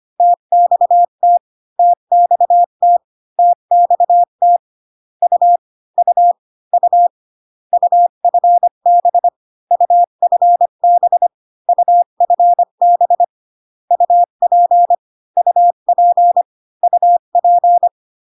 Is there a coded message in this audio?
Yes: Morse code from 0.2 to 17.9 s, heard as a keyed tone, about 25 wpm, 700 Hz. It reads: TXT TXT TXT U U U UFB UFB UFB UP UP UP